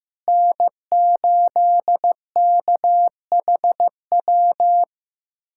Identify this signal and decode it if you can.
Morse code, heard as a keyed tone, about 15 wpm, 700 Hz. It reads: N8KHW
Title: USA Callsign DE Callsign - 5 Characters - 15wpm